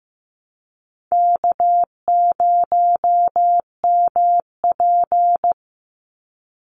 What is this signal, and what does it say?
Morse code, heard as a keyed tone, about 15 wpm, 700 Hz. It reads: K0MP
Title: NA SOTA Chaser Callsigns 15wpm